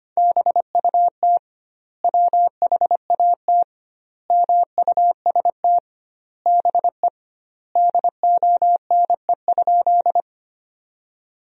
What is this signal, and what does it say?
Morse code, heard as a keyed tone, about 25 wpm, 700 Hz. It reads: BUT WHAT MUST BE DONE?
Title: 2x Word-Spacing - Sentences from Top 500 Words - 25wpm